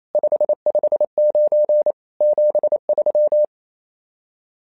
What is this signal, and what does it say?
Morse code, heard as a keyed tone, about 27 wpm, 600 Hz. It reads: 559 73
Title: SOTA QSO Practice 28wpm